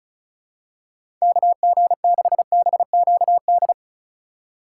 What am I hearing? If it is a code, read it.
Morse code, heard as a keyed tone, about 36 wpm, 700 Hz. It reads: KG6BQD
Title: USA Call Signs - 6 Characters - 4x Character Spacing - 35wpm